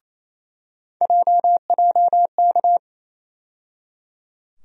Morse code, heard as a keyed tone, about 28 wpm, 700 Hz. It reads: JJK
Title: USA Call Sign Postfixes - 3 Letters 28wpm